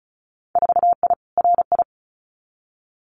Morse code, heard as a keyed tone, about 36 wpm, 700 Hz. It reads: 4I RI
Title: Winter Field Day Exchanges Encore 35wpm